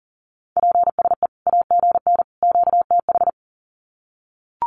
Morse code, heard as a keyed tone, about 40 wpm, 700 Hz. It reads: PSE AGN QTH
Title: Common QSO Phrases 40wpm